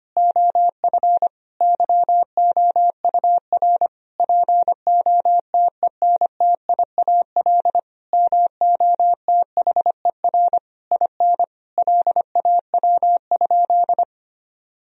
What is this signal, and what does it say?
Morse code, heard as a keyed tone, about 25 wpm, 700 Hz. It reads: OF YOUR POTENTIAL MOTHER IN LAW?
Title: Sentences from Top 1000 Words - Review Part 4 - 25wpm